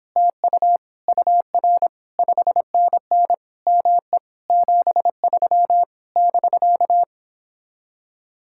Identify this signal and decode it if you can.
Morse code, heard as a keyed tone, about 31 wpm, 700 Hz. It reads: TU UR 5NN ME 73 <BK>